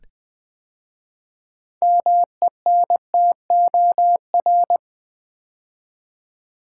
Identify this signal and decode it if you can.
Morse code, heard as a keyed tone, about 20 wpm, 700 Hz. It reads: MENTOR
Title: 6 Letter Words - First 3 Letters Given - 20wpm